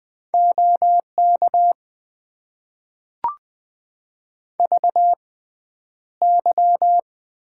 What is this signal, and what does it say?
Morse code, heard as a keyed tone, about 20 wpm, spaced as for 10 wpm, 700 Hz. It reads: OK VY
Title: Top 90 QSO Elements - 6x Character Spacing - 20wpm